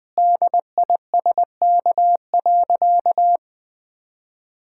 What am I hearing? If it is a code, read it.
Morse code, heard as a keyed tone, about 20 wpm, 700 Hz. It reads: DISK.
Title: Sentences from Top 1000 Words - Review Part 6 - 20wpm